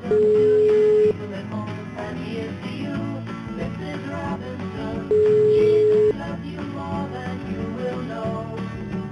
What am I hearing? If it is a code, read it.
Ringback tone (Europe, 425 Hz), with noise.